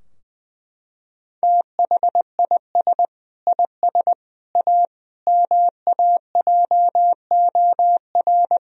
Morse code, heard as a keyed tone, about 20 wpm, 700 Hz. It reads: THIS IS A MAJOR
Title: Sentences from Top 1000 Words - Review Part 4 - 20wpm